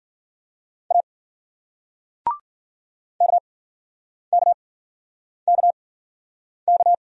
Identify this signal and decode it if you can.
Morse code, heard as a keyed tone, about 53 wpm, spaced as for 9 wpm, 700 Hz. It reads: A KKKK